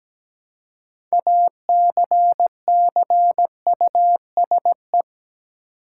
Morse code, heard as a keyed tone, about 17 wpm, 700 Hz. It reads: ACCUSE
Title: Six Letter Words - 17wpm